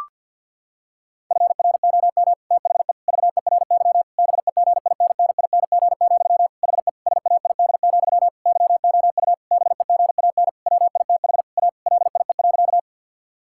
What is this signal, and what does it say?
Morse code, heard as a keyed tone, about 50 wpm, 700 Hz. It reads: FROM THE VERY BEGINNING, HE SAID, YOU BEGAN WITH A LIE.
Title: Sentences from Top 1600 Words 50wpm